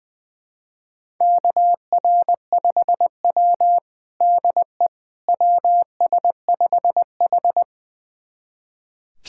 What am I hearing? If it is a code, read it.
Morse code, heard as a keyed tone, about 20 wpm, 700 Hz. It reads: KR5W DE WS5H
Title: USA Callsign DE Callsign - 4 Characters - 20wpm